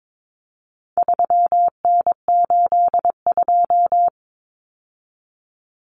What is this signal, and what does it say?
Morse code, heard as a keyed tone, about 22 wpm, 700 Hz. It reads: 3N82
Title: Common Cut Numbers - 4-Digit 22wpm